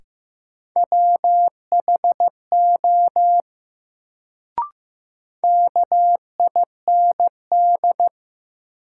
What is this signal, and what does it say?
Morse code, heard as a keyed tone, about 15 wpm, 700 Hz. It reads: WHO KIND
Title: Sets of 2 Words - New Word Drill - 15wpm